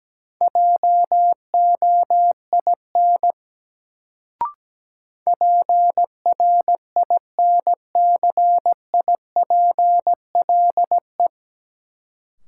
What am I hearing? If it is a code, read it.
Morse code, heard as a keyed tone, about 17 wpm, 700 Hz. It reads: JOIN PRINCIPLE
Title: Sentences from Top 1400 Words - New Word Drill Alt - 17wpm